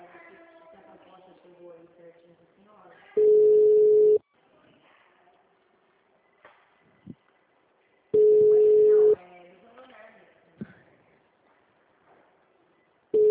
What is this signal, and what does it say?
Ringback tone (Europe, 425 Hz), with speech.
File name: out- -2022- - - .7920